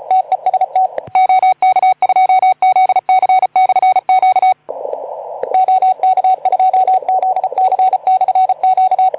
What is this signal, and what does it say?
Morse code, heard as a keyed tone, about 37 wpm, 750 Hz. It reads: TEST OK2ZC/Q OK2ZC/Q
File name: Archiv / CUC181 / OK2ZC CUC 181